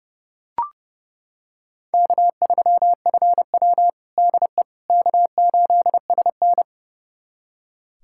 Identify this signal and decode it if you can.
Morse code, heard as a keyed tone, about 30 wpm, 700 Hz. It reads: K3FW DE K8SN
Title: USA Callsign DE Callsign - 4 Characters - 30wpm